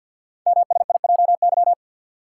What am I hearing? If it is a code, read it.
Morse code, heard as a keyed tone, about 50 wpm, 700 Hz. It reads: MSIJY
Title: Code Groups - 5 Letters - 50wpm